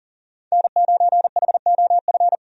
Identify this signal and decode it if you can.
Morse code, heard as a keyed tone, about 37 wpm, 700 Hz. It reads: N9HOF
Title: USA Call Signs - 5 Characters - 40wpm